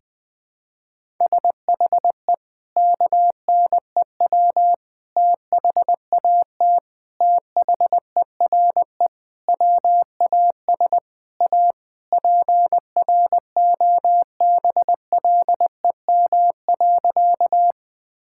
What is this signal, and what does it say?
Morse code, heard as a keyed tone, about 20 wpm, 700 Hz. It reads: SHE KNEW THAT THERE WAS A PROBLEM.
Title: Sentences with 4-Word Phrases - Spoken Beginning - 20wpm